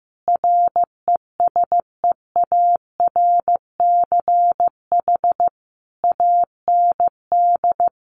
Morse code, heard as a keyed tone, about 15 wpm, 700 Hz. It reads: RESEARCH AND
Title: Sentences from Top 1400 Words 15wpm